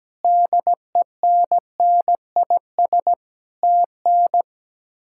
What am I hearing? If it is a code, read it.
Morse code, heard as a keyed tone, about 17 wpm, 700 Hz. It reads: DENNIS TN
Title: K1USN SST QSO Practice - 17wpm